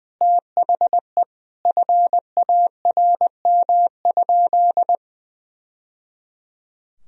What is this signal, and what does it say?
Morse code, heard as a keyed tone, about 20 wpm, 700 Hz. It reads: THE FARM?
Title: Sentences from Top 1000 Words - Review Part 2 - 20wpm